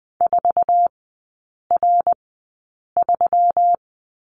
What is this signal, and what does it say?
Morse code, heard as a keyed tone, about 20 wpm, spaced as for 8 wpm, 700 Hz. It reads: 4R3
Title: 121 Lesson 20e - Number 3 - Three Characters 20wpm